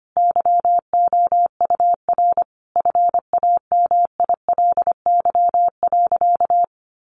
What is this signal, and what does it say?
Morse code, heard as a keyed tone, about 25 wpm, 700 Hz. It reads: YOUR FAMILY.
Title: Sentences from Top 1400 Words 25wpm